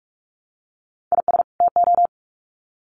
Morse code, heard as a keyed tone, about 44 wpm, 700 Hz. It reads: IS TO
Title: Sets of 2 Words 45wpm